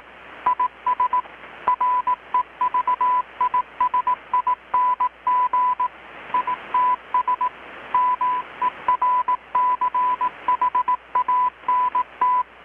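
Morse code, heard as a keyed tone, about 18 wpm, 1000 Hz. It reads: IS REVISING ITS MERCHANT
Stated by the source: Morse from KPH's Night of Nights XX maritime radiotelegraphy event, received in Chiapas, Mexico about 3,800 km away on a Kenwood R-600 with a grounded 10 m long-wire antenna.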